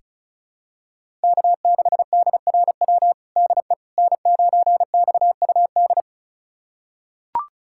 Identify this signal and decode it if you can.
Morse code, heard as a keyed tone, about 36 wpm, 700 Hz. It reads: K6DRW DE N9XUD
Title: USA Callsign DE Callsign - 5 Characters - 35wpm